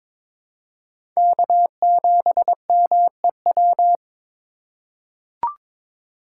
Morse code, heard as a keyed tone, about 22 wpm, 700 Hz. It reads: K7MEW
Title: USA Call Signs - 2x Character Spacing - 22wpm